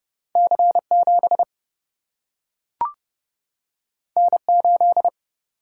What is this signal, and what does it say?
Morse code, heard as a keyed tone, about 30 wpm, 700 Hz. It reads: C 7 N 8
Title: Single-Letter Single-Number 30wpm